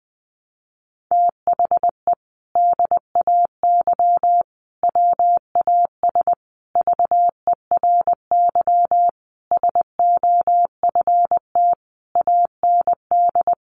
Morse code, heard as a keyed tone, about 20 wpm, 700 Hz. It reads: THE DAY WAS VERY SOFT AND
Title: Sentences from Top 1500 Words 20wpm